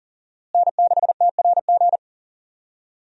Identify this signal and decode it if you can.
Morse code, heard as a keyed tone, about 40 wpm, 700 Hz. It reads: N6TRG
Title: USA Call Signs - 5 Characters - Repeated 3x - 40wpm